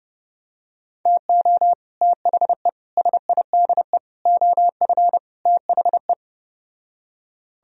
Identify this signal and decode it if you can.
Morse code, heard as a keyed tone, about 30 wpm, 700 Hz. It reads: TO THE SIDE OF THE